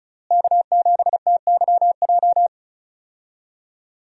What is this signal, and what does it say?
Morse code, heard as a keyed tone, about 35 wpm, 700 Hz. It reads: K7TYJ
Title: USA Call Signs - 5 Characters - Repeated 2x - 35wpm